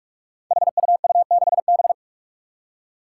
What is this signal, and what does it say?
Morse code, heard as a keyed tone, about 44 wpm, 700 Hz. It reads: HUU6B
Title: Code Groups - 5 Letter:Numbers - 45wpm